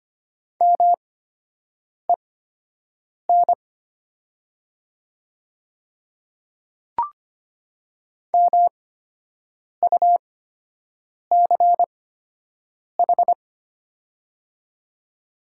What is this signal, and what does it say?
Morse code, heard as a keyed tone, about 25 wpm, spaced as for 7 wpm, 690 Hz. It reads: MEN MUCH
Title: 8x Character Spacing - Top 300 Words - 25wpm